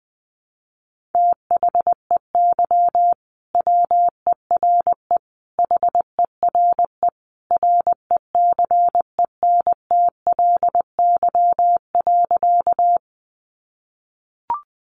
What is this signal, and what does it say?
Morse code, heard as a keyed tone, about 20 wpm, 700 Hz. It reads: THEY WERE HERE RECENTLY.
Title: Sentences from Top 1200 Words 20wpm